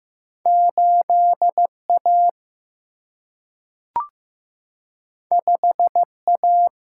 Morse code, heard as a keyed tone, about 15 wpm, 700 Hz. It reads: 8 A 5 A
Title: Single-Letter Single-Number 15wpm